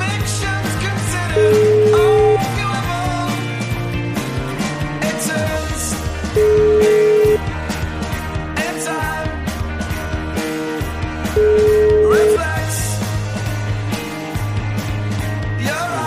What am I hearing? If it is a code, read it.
Ringback tone (Europe, 425 Hz), with noise.